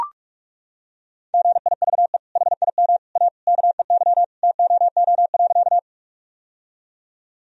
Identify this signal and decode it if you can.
Morse code, heard as a keyed tone, about 41 wpm, 700 Hz. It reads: GIVE HIM A KEY TOO.
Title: Sentences from Top 1000 Words - Review Part 4 - 45wpm